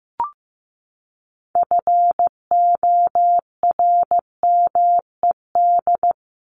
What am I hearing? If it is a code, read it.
Morse code, heard as a keyed tone, about 15 wpm, 700 Hz. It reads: FORMED
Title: ED Suffix - Top 100 Words with Base Words 15wpm